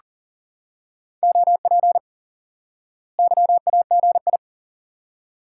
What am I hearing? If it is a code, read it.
Morse code, heard as a keyed tone, about 40 wpm, 700 Hz. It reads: OP YAGI
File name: IQE Course 3 - Lesson 89 - YAGI - Two Elements - 40wpm